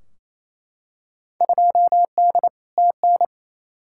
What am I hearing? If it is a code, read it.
Morse code, heard as a keyed tone, about 28 wpm, 700 Hz. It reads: 2D TN
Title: Field Day Exchanges Encore - 28wpm